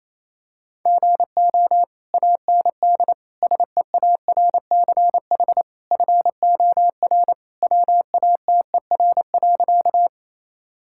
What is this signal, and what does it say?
Morse code, heard as a keyed tone, about 28 wpm, 700 Hz. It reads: GO AND SEARCH FOR WATER.
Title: Sentences from Top 1000 Words - Review Part 5 - 28wpm